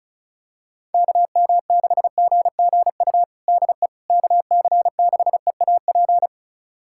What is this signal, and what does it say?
Morse code, heard as a keyed tone, about 34 wpm, 700 Hz. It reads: KM6GGU DE KC6EAP
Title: USA Callsign DE Callsign 35wpm